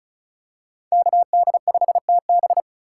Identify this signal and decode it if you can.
Morse code, heard as a keyed tone, about 36 wpm, 700 Hz. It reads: KD5TB
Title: USA Call Signs - 5 Characters - 6x Character Spacing - 35wpm